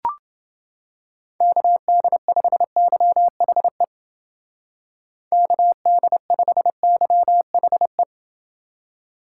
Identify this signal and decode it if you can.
Morse code, heard as a keyed tone, about 29 wpm, 700 Hz. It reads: KD5YHE KD5YHE